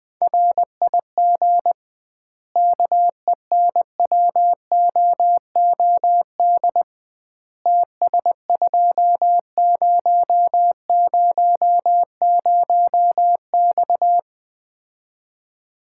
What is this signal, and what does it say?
Morse code, heard as a keyed tone, about 20 wpm, 700 Hz. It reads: RIG KENWOOD TS2000X
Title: Common QSO Phrases - 2x Word-Spacing - 20wpm